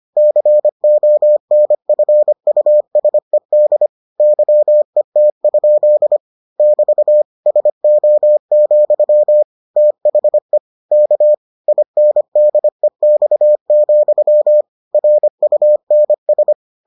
Morse code, heard as a keyed tone, about 25 wpm, 600 Hz. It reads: CONFUSED YET? <BT> SO, THE K INDEX, RUNS